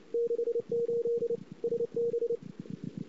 Morse code, heard as a keyed tone, about 26 wpm, 470 Hz. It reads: 68 SB
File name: CW / 2008-ss-cw-k5tr-op-n5rz -22:00